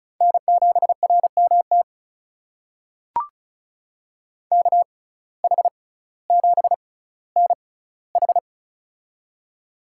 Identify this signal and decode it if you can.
Morse code, heard as a keyed tone, about 36 wpm, spaced as for 18 wpm, 700 Hz. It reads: N7RMT KH7NH
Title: USA Call Signs - 6x Character Spacing - 35wpm